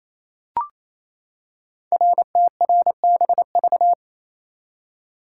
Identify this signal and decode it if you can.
Morse code, heard as a keyed tone, about 28 wpm, 700 Hz. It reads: RTRBV
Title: Code Groups - 5 Letters - 28wpm